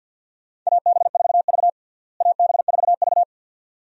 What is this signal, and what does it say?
Morse code, heard as a keyed tone, about 47 wpm, 690 Hz. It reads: AB4V AB4V
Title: USA Call Signs - 4 Characters - Repeated 2x - 50wpm